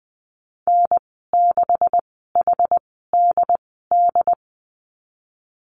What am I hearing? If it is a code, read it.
Morse code, heard as a keyed tone, about 20 wpm, spaced as for 15 wpm, 700 Hz. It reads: N6HDD
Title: 128 Lesson 21g - Number 6 - Call Signs 20wpm